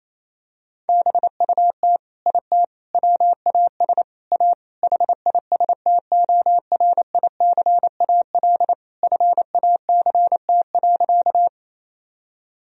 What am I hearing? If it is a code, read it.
Morse code, heard as a keyed tone, about 28 wpm, 700 Hz. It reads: BUT IT WAS A HISTORICAL FACT.
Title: Sentences from Top 1000 Words - Review Part 4 - 28wpm